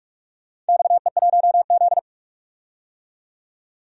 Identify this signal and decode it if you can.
Morse code, heard as a keyed tone, about 44 wpm, 700 Hz. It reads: XE1Z
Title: DXCC-100 Call Signs - 4 Characters - 45wpm